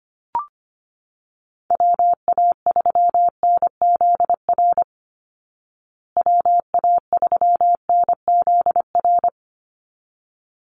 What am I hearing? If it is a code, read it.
Morse code, heard as a keyed tone, about 25 wpm, 700 Hz. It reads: WA3NZR WA3NZR
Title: Call Signs Repeated 3x - 25wpm